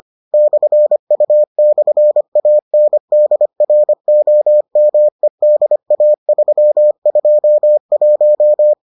Morse code, heard as a keyed tone, about 25 wpm, 600 Hz. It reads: /U/ANDROMEDA321